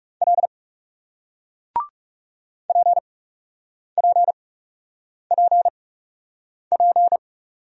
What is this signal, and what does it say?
Morse code, heard as a keyed tone, about 38 wpm, spaced as for 8 wpm, 700 Hz. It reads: L PPPP